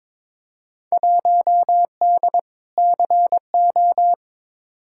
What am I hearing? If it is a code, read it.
Morse code, heard as a keyed tone, about 22 wpm, 700 Hz. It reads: 1D CO
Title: Field Day Exchanges - 22wpm